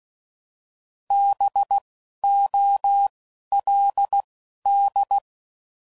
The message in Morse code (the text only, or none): B O L D